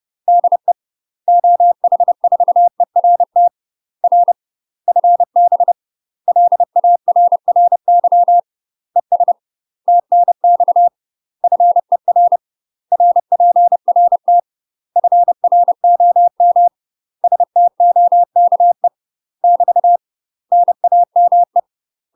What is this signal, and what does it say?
Morse code, heard as a keyed tone, about 30 wpm, 700 Hz. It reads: DE OH4ERT R FB LARRY ES TNX FER RPRT FROM STOKE <BT> NAME
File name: UK MASTER QSOs TRAINING / 30WPM / UK MASTER QSOs 0050